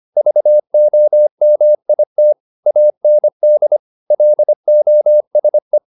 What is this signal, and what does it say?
Morse code, heard as a keyed tone, about 25 wpm, 600 Hz. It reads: VOMIT AND LOSE